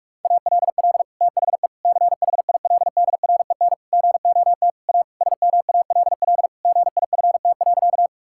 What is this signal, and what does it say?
Morse code, heard as a keyed tone, about 46 wpm, 700 Hz. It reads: ALL THE CHILDREN GOT A SMALL GIFT.
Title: Sentences from Top 1000 Words - Review Part 6 - 45wpm